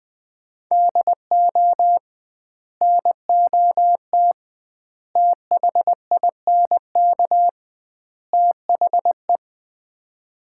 Morse code, heard as a keyed tone, about 21 wpm, 700 Hz. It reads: DO NOT THINK THE
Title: 2x Word-Spacing - Sets of 4 Words - 20wpm